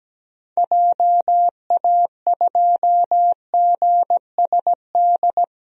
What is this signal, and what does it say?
Morse code, heard as a keyed tone, about 17 wpm, 700 Hz. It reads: JA2GSD
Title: DXCC-100 Call Signs - 6 Characters - Repeated 2x - 17wpm